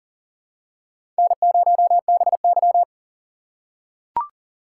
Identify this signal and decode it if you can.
Morse code, heard as a keyed tone, about 40 wpm, 700 Hz. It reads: N0BY
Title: USA Call Signs - 4 Characters - 3x Character Spacing - 40wpm